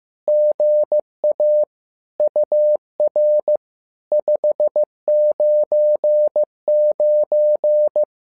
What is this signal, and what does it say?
Morse code, heard as a keyed tone, about 15 wpm, 600 Hz. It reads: GA UR 599
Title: SOTA QSO Practice 15wpm